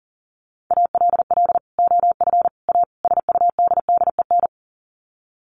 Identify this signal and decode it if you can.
Morse code, heard as a keyed tone, about 40 wpm, 700 Hz. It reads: ALL OF A SUDDEN
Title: Sets of 4 Words 40wpm